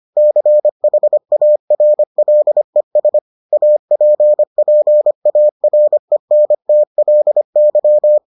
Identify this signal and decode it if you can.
Morse code, heard as a keyed tone, about 25 wpm, 600 Hz. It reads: CHARLES APPARENTLY